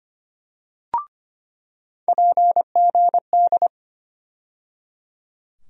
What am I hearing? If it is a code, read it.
Morse code, heard as a keyed tone, about 25 wpm, 700 Hz. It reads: PGD
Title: USA Call Sign Postfixes - 3 Letters 25wpm